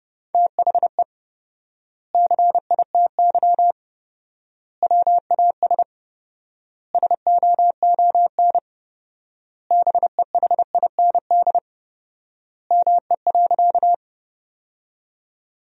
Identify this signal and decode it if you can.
Morse code, heard as a keyed tone, about 30 wpm, 700 Hz. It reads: THE CITY WAS SOON BEHIND ME.
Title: 4x Word-Spacing - Sentences from Top 500 Words - 30wpm